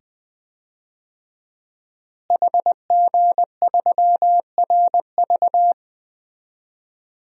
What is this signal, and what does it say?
Morse code, heard as a keyed tone, about 20 wpm, 700 Hz. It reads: HG3RV
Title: DXCC-100 Call Signs - 5 Characters - Repeated 2x - 20wpm